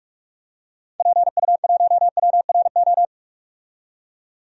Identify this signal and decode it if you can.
Morse code, heard as a keyed tone, about 46 wpm, 700 Hz. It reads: PU1WRO
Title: DXCC-100 Call Signs - 6 Characters - 45wpm